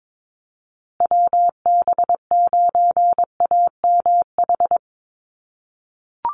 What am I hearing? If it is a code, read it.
Morse code, heard as a keyed tone, about 22 wpm, 700 Hz. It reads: WB9AMH
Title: USA Call Signs Repeated 3x 22wpm